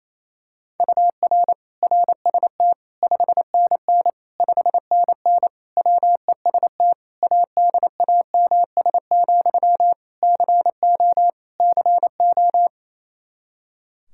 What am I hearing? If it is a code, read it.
Morse code, heard as a keyed tone, about 28 wpm, 700 Hz. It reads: UR RST 5NN 5NN WEST ADAMS, CO CO
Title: K1USN SST QSO Practice 28wpm